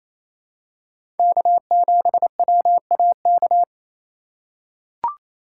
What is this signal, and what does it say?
Morse code, heard as a keyed tone, about 28 wpm, 700 Hz. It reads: K7WAK
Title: USA Call Signs - 5 Characters - 2x Character Spacing - 28wpm